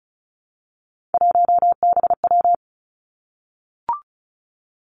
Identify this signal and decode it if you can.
Morse code, heard as a keyed tone, about 36 wpm, 700 Hz. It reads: 1BW
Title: Three Challenging Letter-Number Combinations 35wpm